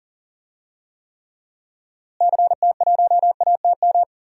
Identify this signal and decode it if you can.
Morse code, heard as a keyed tone, about 40 wpm, 700 Hz. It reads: CT1ATM
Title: DXCC-100 Call Signs - 6 Characters - Repeated 2x - 40wpm